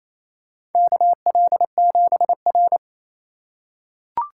Morse code, heard as a keyed tone, about 27 wpm, 700 Hz. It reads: KL7R
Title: USA Call Signs - 4 Characters - 3x Character Spacing - 28wpm